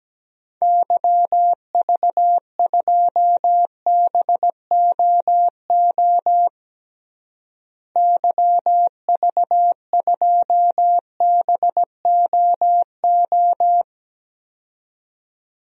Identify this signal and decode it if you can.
Morse code, heard as a keyed tone, about 17 wpm, 700 Hz. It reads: YV2BOO YV2BOO
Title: DXCC-100 Call Signs - 6 Characters - Repeated 2x - 17wpm